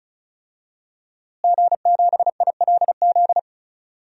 Morse code, heard as a keyed tone, about 34 wpm, 700 Hz. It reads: G7ILZ
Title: DXCC-100 Call Signs - 5 Characters - Repeated 3x - 35wpm